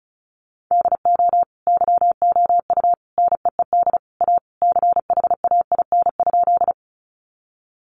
Morse code, heard as a keyed tone, about 34 wpm, 700 Hz. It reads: DO YOU NEED A CHAIN?
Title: Sentences from Top 1500 Words 35wpm